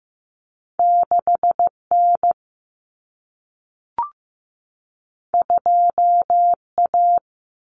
Morse code, heard as a keyed tone, about 15 wpm, 700 Hz. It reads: 6N 2A
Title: Common Cut Numbers - 2-Digit 15wpm